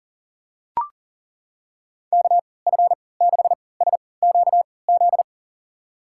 Morse code, heard as a keyed tone, about 40 wpm, spaced as for 23 wpm, 700 Hz. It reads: KF6SQZ
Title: USA Call Signs - 6 Characters - 3x Character Spacing - 40wpm